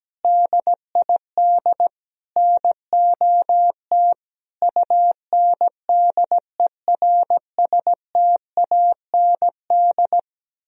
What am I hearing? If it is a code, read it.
Morse code, heard as a keyed tone, about 17 wpm, 700 Hz. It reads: DID NOT UNDERSTAND
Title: Sentences from Top 1000 Words - Review Part 3 - 17wpm